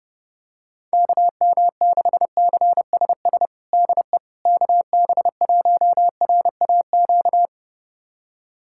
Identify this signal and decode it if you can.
Morse code, heard as a keyed tone, about 32 wpm, 700 Hz. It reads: KM6CSS DE KB1RAQ
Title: USA Callsign DE Callsign 30wpm